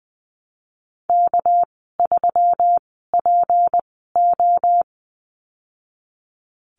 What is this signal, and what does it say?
Morse code, heard as a keyed tone, about 20 wpm, spaced as for 15 wpm, 700 Hz. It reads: K3PO
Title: 231 Lesson 37g - Letter x - Call Signs 20wpm